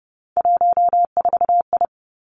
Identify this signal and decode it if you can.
Morse code, heard as a keyed tone, about 30 wpm, 700 Hz. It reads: 14I
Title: Three Challenging Letter-Number Combinations 30wpm